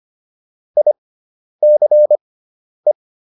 Morse code, heard as a keyed tone, about 25 wpm, spaced as for 10 wpm, 600 Hz. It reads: ICE